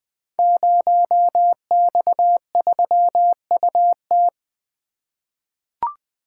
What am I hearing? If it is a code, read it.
Morse code, heard as a keyed tone, about 20 wpm, 700 Hz. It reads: 0X3UT
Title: Code Groups - 5 Letter:Numbers - 20wpm